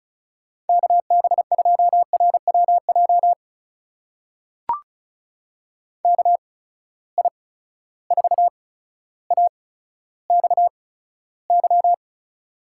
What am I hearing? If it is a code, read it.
Morse code, heard as a keyed tone, about 36 wpm, spaced as for 15 wpm, 700 Hz. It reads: KB2RWJ KI4AXY
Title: USA Call Signs - 6 Characters - 8x Character Spacing - 35wpm